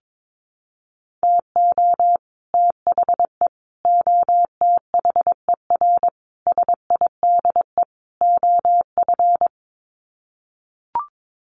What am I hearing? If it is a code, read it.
Morse code, heard as a keyed tone, about 22 wpm, 700 Hz. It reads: TO THE OTHER SIDE OF
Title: Sets of 5 Words Encore 22wpm